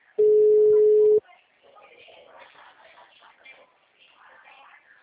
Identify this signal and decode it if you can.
Ringback tone (Europe, 425 Hz), with speech.